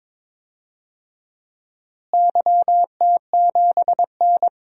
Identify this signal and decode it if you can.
Morse code, heard as a keyed tone, about 22 wpm, 700 Hz. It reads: YT7N
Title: DXCC-100 Call Signs - 4 Characters - Repeated 3x - 22wpm